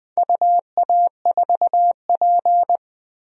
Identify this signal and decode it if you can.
Morse code, heard as a keyed tone, about 20 wpm, 700 Hz. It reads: UA4P
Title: DXCC-100 Call Signs - 4 Characters - 20wpm